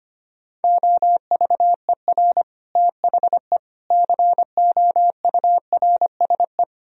Morse code, heard as a keyed tone, about 31 wpm, 700 Hz. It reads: OVER THE COURSE
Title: Speed-Racing - Sets of 4 Words 25wpm